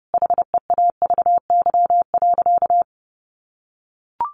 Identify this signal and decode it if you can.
Morse code, heard as a keyed tone, about 30 wpm, 700 Hz. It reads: HEAVY.
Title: Sentences from Top 1100 Words 30wpm